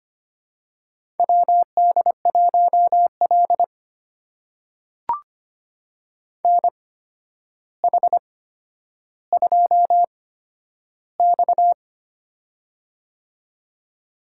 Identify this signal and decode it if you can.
Morse code, heard as a keyed tone, about 25 wpm, spaced as for 11 wpm, 700 Hz. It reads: WD1L NH2X
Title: USA Call Signs - 4 Characters - 8x Character Spacing - 25wpm